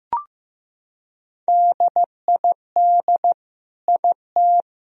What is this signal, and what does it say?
Morse code, heard as a keyed tone, about 15 wpm, 700 Hz. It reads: DID IT
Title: Sentences from Top 1000 Words - Review Part 2 - 15wpm